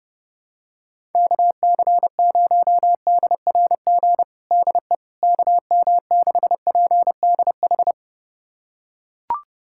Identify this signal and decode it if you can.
Morse code, heard as a keyed tone, about 29 wpm, 700 Hz. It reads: KC0DRG DE KM6PDH
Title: USA Callsign DE Callsign - 6 Characters - 30wpm